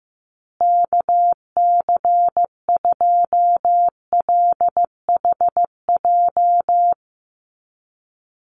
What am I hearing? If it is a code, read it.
Morse code, heard as a keyed tone, about 15 wpm, 700 Hz. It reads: KC2LHJ
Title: USA Call Signs Repeated 2x 15wpm